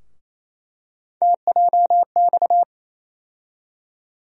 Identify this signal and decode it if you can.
Morse code, heard as a keyed tone, about 28 wpm, 700 Hz. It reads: TJX
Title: USA Call Sign Postfixes - 3 Letters 28wpm